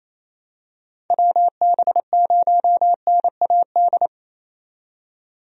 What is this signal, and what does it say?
Morse code, heard as a keyed tone, about 28 wpm, 700 Hz. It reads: WB0NAD
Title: USA Call Signs - 6 Characters - 28wpm